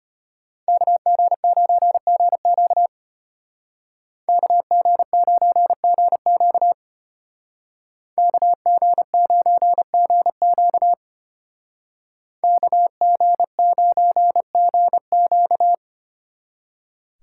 Morse code, heard as a keyed tone, about 33 wpm, 700 Hz. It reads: KG9GQ KG9GQ KG9GQ KG9GQ